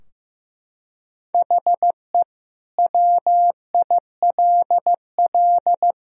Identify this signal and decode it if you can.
Morse code, heard as a keyed tone, about 15 wpm, 700 Hz. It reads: HE WILL
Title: Sentences from Top 1000 Words - Review Part 3 - 15wpm